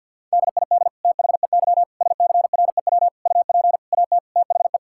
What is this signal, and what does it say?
Morse code, heard as a keyed tone, about 50 wpm, 700 Hz. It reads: DID THEY SCREW UP AT THE
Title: Sentences from Top 1600 Words 50wpm